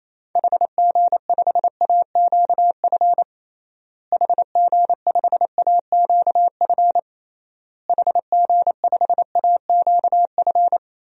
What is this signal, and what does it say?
Morse code, heard as a keyed tone, about 28 wpm, 700 Hz. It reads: HG5AQF HG5AQF HG5AQF
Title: DXCC-100 Call Signs - 6 Characters - Repeated 3x - 28wpm